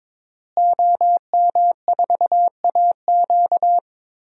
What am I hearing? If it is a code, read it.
Morse code, heard as a keyed tone, about 22 wpm, 700 Hz. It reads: OM4AQ
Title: DXCC-100 Call Signs - 5 Characters - Repeated 3x - 22wpm